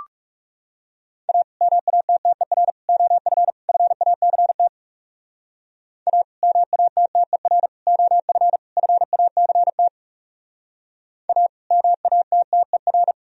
Morse code, heard as a keyed tone, about 37 wpm, 700 Hz. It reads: A MATTER OF FACT A MATTER OF FACT A MATTER
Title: Speed-Racing - Sets of 4 Words 30wpm